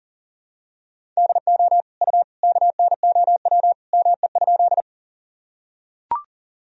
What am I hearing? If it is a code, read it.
Morse code, heard as a keyed tone, about 40 wpm, 700 Hz. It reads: DO U KNOW ME?
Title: Sentences from Top 200 Words and Abbreviations 40wpm